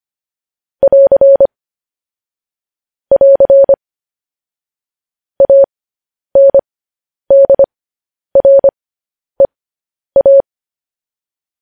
Morse code, heard as a keyed tone, about 25 wpm, spaced as for 10 wpm, 550 Hz. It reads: <AR> <AR> ANDREA
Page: news headlines in morse code at 10 wpm